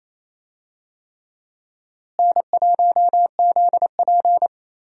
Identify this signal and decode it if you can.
Morse code, heard as a keyed tone, about 28 wpm, 700 Hz. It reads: N1ZP
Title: USA Call Signs - 4 Characters - Repeated 3x - 28wpm